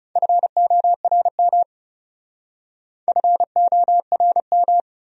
Morse code, heard as a keyed tone, about 36 wpm, 700 Hz. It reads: FORM FORM
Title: Speed-Racing Top 200 Words 30wpm